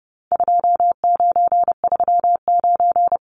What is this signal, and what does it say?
Morse code, heard as a keyed tone, about 30 wpm, 700 Hz. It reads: 2939
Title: Numbers - 4-Digit 30wpm